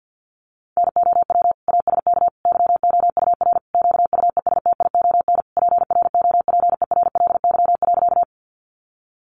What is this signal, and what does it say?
Morse code, heard as a keyed tone, about 50 wpm, 700 Hz. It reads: NOW ASK YOUR QUESTION PROPERLY.
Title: Sentences from Top 1100 Words 50wpm